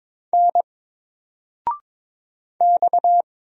Morse code, heard as a keyed tone, about 22 wpm, 700 Hz. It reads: N X
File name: Single-Letter-Number-22wpm